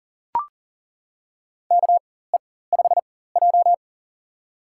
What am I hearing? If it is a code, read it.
Morse code, heard as a keyed tone, about 40 wpm, spaced as for 19 wpm, 700 Hz. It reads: KE5J
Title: USA Call Signs - 4 Characters - 4x Character Spacing - 40wpm